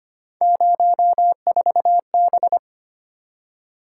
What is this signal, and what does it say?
Morse code, heard as a keyed tone, about 25 wpm, 700 Hz. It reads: 04B
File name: Three Letter-Number Combinations - 25wpm